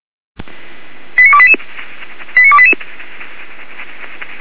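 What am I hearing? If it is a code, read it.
Five-tone selective call (ZVEI), heard as quick runs of five tones, tones of about 70 ms. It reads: 89288 89288
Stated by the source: ZVEI five-tone selective call 89300, SDR recording.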